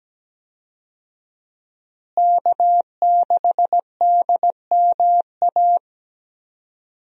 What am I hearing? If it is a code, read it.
Morse code, heard as a keyed tone, about 17 wpm, 700 Hz. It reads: K6DMA
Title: USA Call Signs - 5 Characters - Repeated 2x - 17wpm